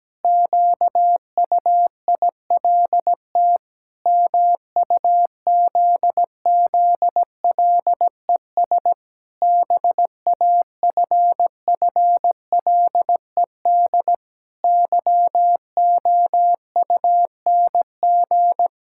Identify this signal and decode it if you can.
Morse code, heard as a keyed tone, about 17 wpm, 700 Hz. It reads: QUILT MUZZLES BAFFLED YOUNG
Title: Pangram Challenge v2 - 17wpm